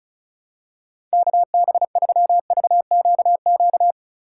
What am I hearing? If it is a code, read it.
Morse code, heard as a keyed tone, about 36 wpm, 700 Hz. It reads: KB3VQQ
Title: USA Call Signs - 6 Characters - 2x Character Spacing - 35wpm